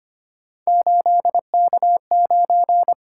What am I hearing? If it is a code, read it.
Morse code, heard as a keyed tone, about 25 wpm, 700 Hz. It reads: 8K9
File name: Three Letter-Number Combinations - 25wpm